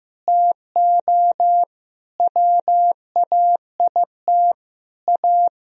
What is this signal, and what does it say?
Morse code, heard as a keyed tone, about 15 wpm, 700 Hz. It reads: TO WAIT A
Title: Sentences from Top 1000 Words - Review Part 2 - 15wpm